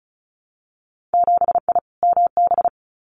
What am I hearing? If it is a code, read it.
Morse code, heard as a keyed tone, about 36 wpm, 700 Hz. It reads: 7I MB
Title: Winter Field Day Exchanges Encore 35wpm